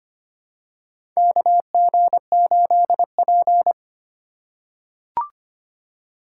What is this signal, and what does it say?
Morse code, heard as a keyed tone, about 25 wpm, 700 Hz. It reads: KG8P
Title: USA Call Signs - 4 Characters - 4x Character Spacing - 25wpm